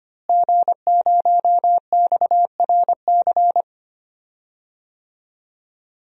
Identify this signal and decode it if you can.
Morse code, heard as a keyed tone, about 25 wpm, 700 Hz. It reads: G0XRC
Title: DXCC-100 Call Signs - 5 Characters - Repeated 2x - 25wpm